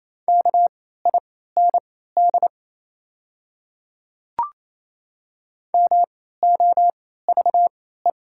Morse code, heard as a keyed tone, about 28 wpm, spaced as for 16 wpm, 700 Hz. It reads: KIND MOVE
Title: 3x Character Spacing - Top 200 Words - 28wpm